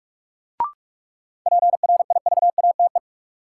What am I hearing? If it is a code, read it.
Morse code, heard as a keyed tone, about 46 wpm, 700 Hz. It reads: PRIVATE
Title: Sentences from Top 1000 Words - New Word Drill - 45wpm